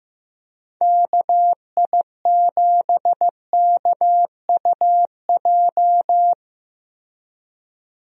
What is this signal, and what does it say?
Morse code, heard as a keyed tone, about 15 wpm, 700 Hz. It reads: KI7KUJ
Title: USA Call Signs - 6 Characters - Repeated 3x - 15wpm